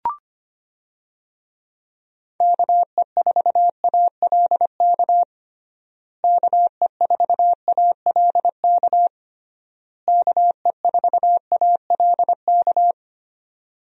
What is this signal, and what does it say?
Morse code, heard as a keyed tone, about 25 wpm, 700 Hz. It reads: KE4ALK KE4ALK KE4ALK
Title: USA Call Signs - 6 Characters - Repeated 3x - 25wpm